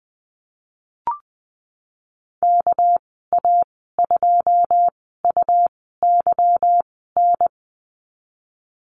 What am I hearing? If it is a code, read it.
Morse code, heard as a keyed tone, about 20 wpm, 700 Hz. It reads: K A 2 U Y N
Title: 238 Lesson 38g - Letter q - Call Signs 20wpm